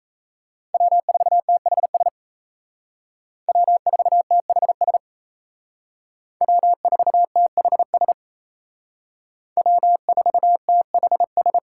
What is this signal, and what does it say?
Morse code, heard as a keyed tone, about 36 wpm, 700 Hz. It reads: W4THS W4THS W4THS W4THS